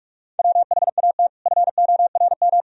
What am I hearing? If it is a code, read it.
Morse code, heard as a keyed tone, about 46 wpm, 700 Hz. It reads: WHAT FORM
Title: Sentences from Top 1000 Words - Review Part 1 - 45wpm